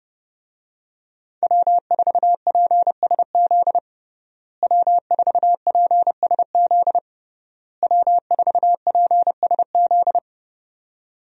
Morse code, heard as a keyed tone, about 31 wpm, 700 Hz. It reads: W4PSZ W4PSZ W4PSZ
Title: USA Call Signs - 5 Characters - Repeated 3x - 30wpm